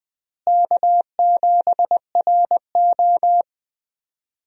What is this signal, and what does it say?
Morse code, heard as a keyed tone, about 21 wpm, 700 Hz. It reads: K7RO
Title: USA Call Signs - 4 Characters - Repeated 2x - 20wpm